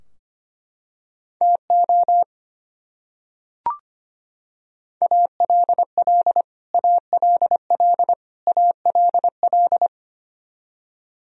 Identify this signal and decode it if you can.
Morse code, heard as a keyed tone, about 25 wpm, 700 Hz. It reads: TO ALL ALL ALL
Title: Top 100 Words Repeated 25wpm